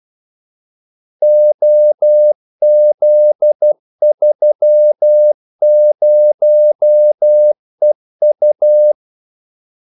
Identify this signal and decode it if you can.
Morse code, heard as a keyed tone, about 12 wpm, 600 Hz. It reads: OZ30EU
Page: 2 ekstra ordmellemrum mellem kaldesignalerne